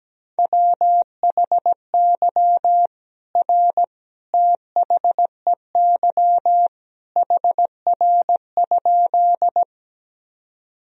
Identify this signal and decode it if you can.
Morse code, heard as a keyed tone, about 17 wpm, 700 Hz. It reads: WHY R THEY HR?
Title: Sentences from Top 200 Words and Abbreviations 17wpm